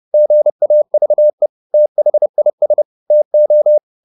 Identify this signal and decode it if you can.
Morse code, heard as a keyed tone, about 30 wpm, 600 Hz. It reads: GAVE THIS TO
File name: the old man and the sea / the old man and the sea 1 30wpm